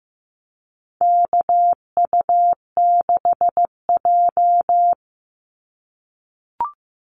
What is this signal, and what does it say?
Morse code, heard as a keyed tone, about 15 wpm, 700 Hz. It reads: KU6J
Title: NA SOTA Activator Callsigns 15wpm